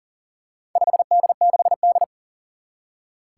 Morse code, heard as a keyed tone, about 44 wpm, 700 Hz. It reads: 5D6D
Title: Four Challenging Letter-Number Combinations 40wpm